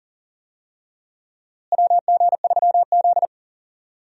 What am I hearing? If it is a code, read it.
Morse code, heard as a keyed tone, about 40 wpm, 700 Hz. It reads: WG3Z
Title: USA Call Signs - 4 Characters - Repeated 2x - 40wpm